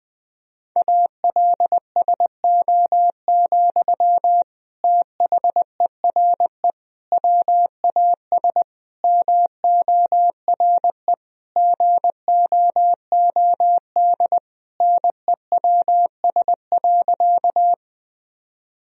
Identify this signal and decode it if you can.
Morse code, heard as a keyed tone, about 19 wpm, 700 Hz. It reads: ALSO, THERE WAS MORE GOOD NEWS.
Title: Sentences from Top 1000 Words - Review Part 1 - 20wpm